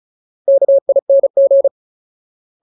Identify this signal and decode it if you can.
Morse code, heard as a keyed tone, about 36 wpm, 550 Hz. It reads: KING
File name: Ultimate Word List / 35WPM Ultimate